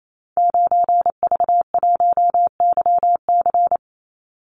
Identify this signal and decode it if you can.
Morse code, heard as a keyed tone, about 28 wpm, 700 Hz. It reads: 9V1YC
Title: Call Signs - 28wpm